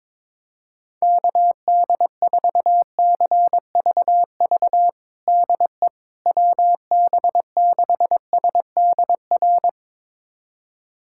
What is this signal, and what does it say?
Morse code, heard as a keyed tone, about 21 wpm, 700 Hz. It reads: KD4CVV DE WB6SDR
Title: USA Callsign DE Callsign - 6 Characters - 22wpm